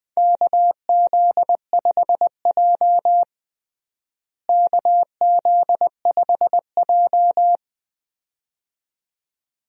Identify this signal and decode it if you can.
Morse code, heard as a keyed tone, about 20 wpm, 700 Hz. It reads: KZ5J KZ5J
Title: USA Call Signs - 4 Characters - Repeated 2x - 20wpm